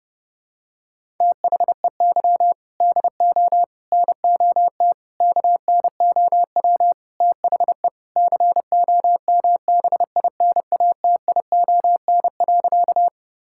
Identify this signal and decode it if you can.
Morse code, heard as a keyed tone, about 30 wpm, 700 Hz. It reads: THEY DO NOT KNOW THE COMBINATION.
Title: Sentences from Top 1000 Words - Review Part 6 - 30wpm